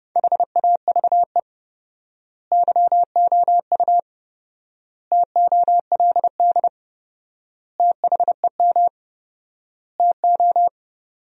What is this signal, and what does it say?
Morse code, heard as a keyed tone, about 29 wpm, 700 Hz. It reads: HAVE YOU TOLD THEM TO
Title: 4x Word-Spacing - Sentences from Top 500 Words - 30wpm